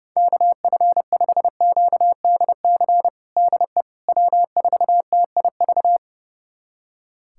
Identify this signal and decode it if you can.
Morse code, heard as a keyed tone, about 30 wpm, 700 Hz. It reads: KF5QDC DE W4TIV
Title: USA Callsign DE Callsign 30wpm